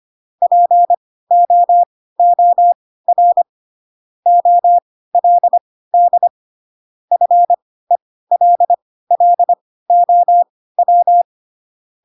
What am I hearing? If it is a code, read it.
Morse code, heard as a keyed tone, about 25 wpm, spaced as for 16 wpm, 700 Hz. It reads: POOR OLD FELLOW
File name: Pinocchio / Pinocchio ch3 10 25 ik1hge